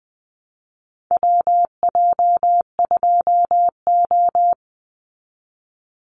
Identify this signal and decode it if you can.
Morse code, heard as a keyed tone, about 20 wpm, 700 Hz. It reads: WJ2O
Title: Call Signs Repeated 3x - 20wpm